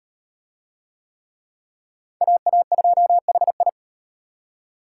Morse code, heard as a keyed tone, about 33 wpm, 700 Hz. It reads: AA2HI